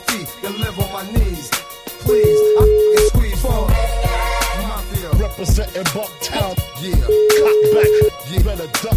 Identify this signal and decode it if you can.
Ringback tone (Europe, 425 Hz), with noise.